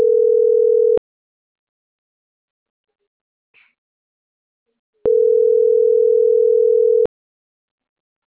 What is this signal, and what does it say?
Ringback tone (North America), 440 + 480 Hz.